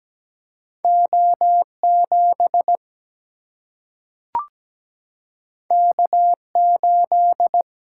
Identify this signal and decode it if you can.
Morse code, heard as a keyed tone, about 17 wpm, 700 Hz. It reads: O7 K8
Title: Single-Letter Single-Number 17wpm